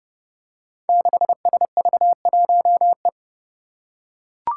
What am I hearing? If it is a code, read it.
Morse code, heard as a keyed tone, about 31 wpm, 700 Hz. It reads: 6SV1E
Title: Five Challenging Letter-Number Combinations 30wpm